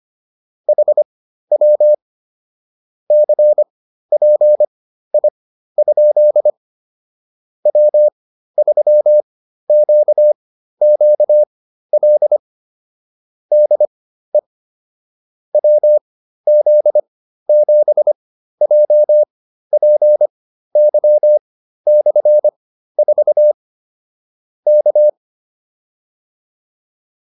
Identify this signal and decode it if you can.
Morse code, heard as a keyed tone, about 25 wpm, spaced as for 13 wpm, 600 Hz. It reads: HW CPI? W3QQL DE WZ7JPY/4 K